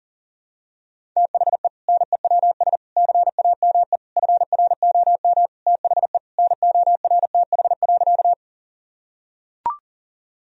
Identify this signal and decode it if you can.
Morse code, heard as a keyed tone, about 40 wpm, 700 Hz. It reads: THE NEWS CAME FROM THE NORTH.
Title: Sentences from Top 1000 Words - Review Part 1 - 40wpm